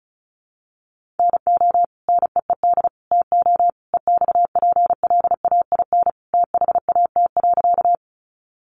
Morse code, heard as a keyed tone, about 34 wpm, 700 Hz. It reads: NO NEED TO EXPLAIN THAT.
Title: Sentences from Top 1100 Words 35wpm